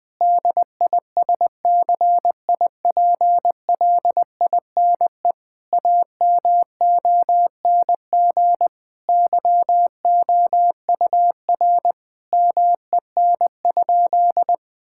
Morse code, heard as a keyed tone, about 20 wpm, 700 Hz. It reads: DISCIPLINE AMONG YOUR MEN?
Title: Sentences from Top 1000 Words - Review Part 6 - 20wpm